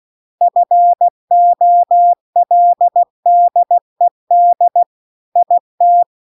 Morse code, heard as a keyed tone, about 16 wpm, 700 Hz. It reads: FOLDED IT
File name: CW-The-Big-Twoheaded-River-2-16WPM